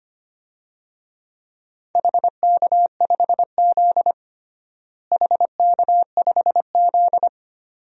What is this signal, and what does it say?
Morse code, heard as a keyed tone, about 25 wpm, 700 Hz. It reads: HK5Z HK5Z
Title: DXCC-100 Call Signs - 4 Characters - Repeated 2x - 25wpm